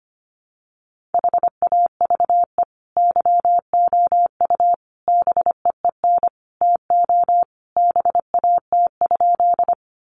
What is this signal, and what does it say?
Morse code, heard as a keyed tone, about 25 wpm, 700 Hz. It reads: HAVE YOU BEEN TO BAT?
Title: Sentences from Top 1500 Words 25wpm